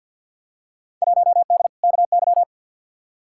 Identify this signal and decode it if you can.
Morse code, heard as a keyed tone, about 51 wpm, 700 Hz. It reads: 1D KY
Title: Field Day Exchanges - 50wpm